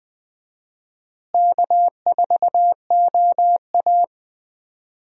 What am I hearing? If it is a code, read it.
Morse code, heard as a keyed tone, about 20 wpm, 700 Hz. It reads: K4OA
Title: USA Call Signs - 4 Characters - Repeated 2x - 20wpm